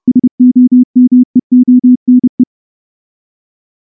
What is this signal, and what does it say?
Morse code, heard as a keyed tone, about 30 wpm, 250 Hz. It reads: SOMEONE
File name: morse-training-30WPM